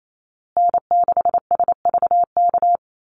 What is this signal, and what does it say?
Morse code, heard as a keyed tone, about 28 wpm, 700 Hz. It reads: N6SVK
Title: USA Call Signs Repeated 2x 28wpm